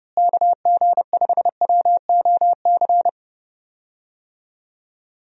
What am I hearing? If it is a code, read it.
Morse code, heard as a keyed tone, about 30 wpm, 700 Hz. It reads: KG5WOC
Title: USA Call Signs - 6 Characters - Repeated 3x - 30wpm